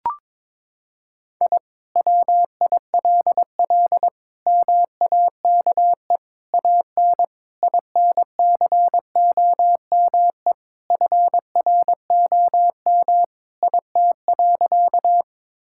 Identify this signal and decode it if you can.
Morse code, heard as a keyed tone, about 22 wpm, 700 Hz. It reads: I WILL MAKE AN INCOME FROM IT.
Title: Sentences from Top 1000 Words - Review Part 5 - 22wpm